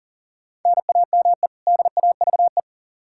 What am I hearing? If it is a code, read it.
Morse code, heard as a keyed tone, about 40 wpm, 700 Hz. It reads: NAME DAVE
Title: Practice QSO - Exchange 1 - Beginning - 40wpm